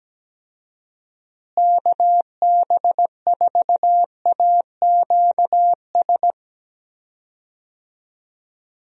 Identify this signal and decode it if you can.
Morse code, heard as a keyed tone, about 17 wpm, 700 Hz. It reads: KB4AQS
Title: USA Call Signs - 6 Characters - Repeated 2x - 17wpm